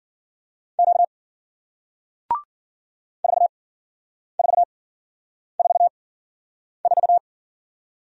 Morse code, heard as a keyed tone, about 54 wpm, spaced as for 9 wpm, 700 Hz. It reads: Q 4444